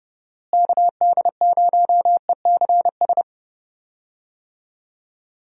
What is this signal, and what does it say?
Morse code, heard as a keyed tone, about 30 wpm, 700 Hz. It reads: KD0ECS
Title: USA Call Signs - 6 Characters - Repeated 3x - 30wpm